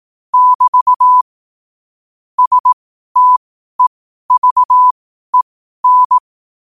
Morse code, heard as a keyed tone, about 18 wpm, spaced as for 12 wpm, 1000 Hz. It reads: <BT> STEVEN
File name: qotdfiles / 12wpm